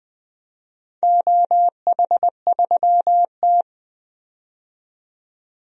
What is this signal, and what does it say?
Morse code, heard as a keyed tone, about 21 wpm, 700 Hz. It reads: OH3T
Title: DXCC-100 Call Signs - 4 Characters - 20wpm